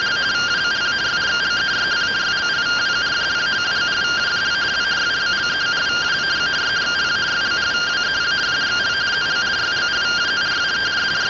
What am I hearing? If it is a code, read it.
Two-tone teleprinter signal, about 170 Hz shift.
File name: findings / CT AXIAL -142.gen-10 rtty rx